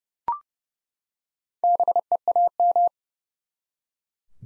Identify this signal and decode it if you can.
Morse code, heard as a keyed tone, about 30 wpm, 700 Hz. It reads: BEAM
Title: Top 90 Abbr 30WPM